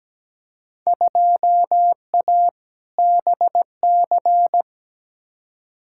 Morse code, heard as a keyed tone, about 17 wpm, 700 Hz. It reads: 2A BC
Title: Field Day Exchanges Encore - 17wpm